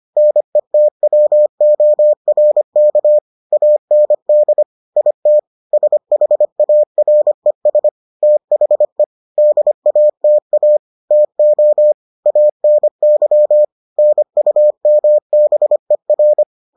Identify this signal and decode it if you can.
Morse code, heard as a keyed tone, about 25 wpm, 600 Hz. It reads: NETWORK AND IT SHARES THE DATA TO ANY NUMBER